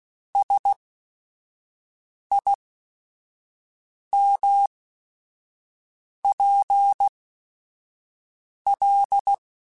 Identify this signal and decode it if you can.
Morse code, heard as a keyed tone, about 16 wpm, spaced as for 5 wpm, 800 Hz. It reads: SIMPL
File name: Technical Subjects / Technical Subjects-mp3s / Sw Pwr Sup @5